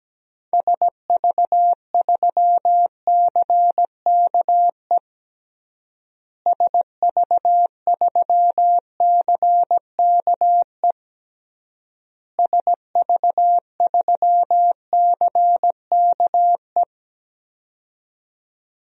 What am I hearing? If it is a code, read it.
Morse code, heard as a keyed tone, about 17 wpm, 700 Hz. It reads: SV3CKE SV3CKE SV3CKE
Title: DXCC-100 Call Signs - 6 Characters - Repeated 3x - 17wpm